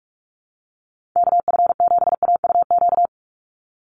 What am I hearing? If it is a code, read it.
Morse code, heard as a keyed tone, about 44 wpm, 700 Hz. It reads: KF7AUQ
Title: USA Call Signs 45wpm